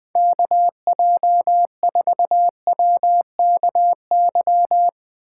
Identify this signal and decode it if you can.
Morse code, heard as a keyed tone, about 20 wpm, 700 Hz. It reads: KJ4WKY
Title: Speed Racing - USA 2x3 Call Signs 17wpm